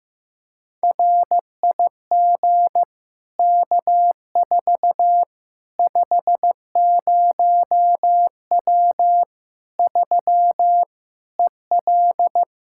Morse code, heard as a keyed tone, about 15 wpm, 700 Hz. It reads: RIG K4 50W 3 EL
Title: Practice QSO - Exchange 2 - Info - 15wpm